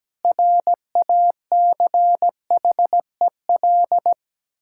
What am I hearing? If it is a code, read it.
Morse code, heard as a keyed tone, about 17 wpm, 700 Hz. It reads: RACHEL
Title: Practice QSO - Exchange 1 - Beginning - 17wpm